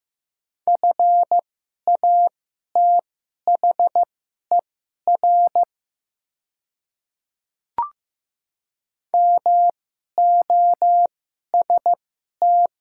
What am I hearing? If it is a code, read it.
Morse code, heard as a keyed tone, about 15 wpm, spaced as for 11 wpm, 700 Hz. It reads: FATHER MOST
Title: 2x Character Spacing - Top 300 Words - 15wpm